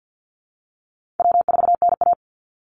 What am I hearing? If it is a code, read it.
Morse code, heard as a keyed tone, about 48 wpm, 700 Hz. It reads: W4NA
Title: Easy Call Signs Repeated 3x - 50wpm